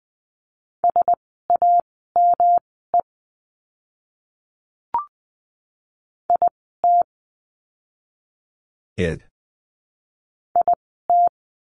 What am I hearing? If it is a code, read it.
Morse code, heard as a keyed tone, about 20 wpm, spaced as for 14 wpm, 700 Hz. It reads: SAME IT IT
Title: 206 Lesson 33f - Letter j - Words 20wpm